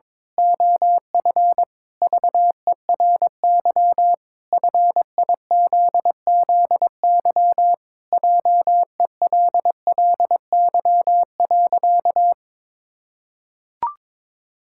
Morse code, heard as a keyed tone, about 22 wpm, 700 Hz. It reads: OF VERY FIZZY JELLY.
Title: Pangram Challenge v2 - 22wpm